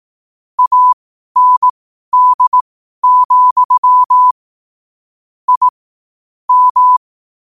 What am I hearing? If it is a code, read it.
Morse code, heard as a keyed tone, about 18 wpm, spaced as for 12 wpm, 1000 Hz. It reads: AND, I M
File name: qotdfiles / 12wpm